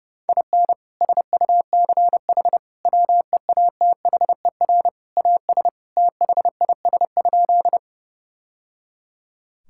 Morse code, heard as a keyed tone, about 30 wpm, 700 Hz. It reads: IN SUCH WEATHER AS THIS?
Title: Sentences from Top 1000 Words - Review Part 5 - 30wpm